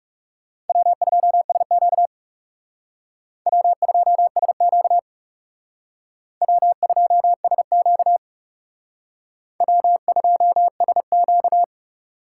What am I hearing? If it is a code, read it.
Morse code, heard as a keyed tone, about 36 wpm, 700 Hz. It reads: W2SQ W2SQ W2SQ W2SQ
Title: Speed Racing - USA 1x2 Call Signs 30wpm